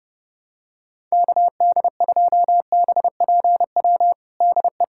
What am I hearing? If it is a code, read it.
Morse code, heard as a keyed tone, about 30 wpm, 700 Hz. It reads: KD2BPW DE
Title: USA Callsign DE Callsign 30wpm